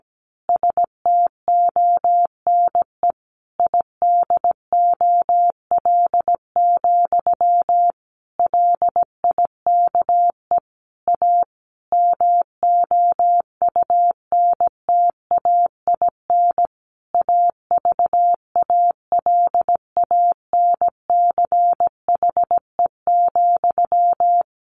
Morse code, heard as a keyed tone, about 17 wpm, 700 Hz. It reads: STONE IDOL, LIKE A MOUNTAIN AVALANCHE,
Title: A Sound Of Thunder 17wpm